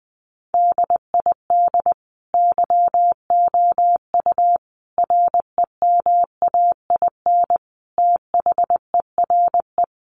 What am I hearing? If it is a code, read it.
Morse code, heard as a keyed tone, about 20 wpm, 700 Hz. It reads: DID YOU REMAIN THERE
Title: Sentences from Top 1200 Words 20wpm